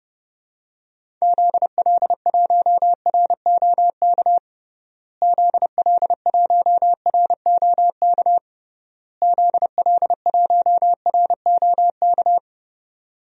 Morse code, heard as a keyed tone, about 30 wpm, 700 Hz. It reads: ZL1ROK ZL1ROK ZL1ROK
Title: DXCC-100 Call Signs - 6 Characters - Repeated 3x - 30wpm